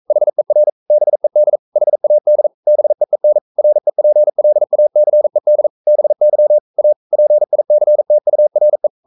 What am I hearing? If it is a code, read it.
Morse code, heard as a keyed tone, about 42 wpm, 600 Hz. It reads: HER BED HAD BEEN REPLACED BY A PICTURE
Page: Romanen Never af Ken Follet på engelsk.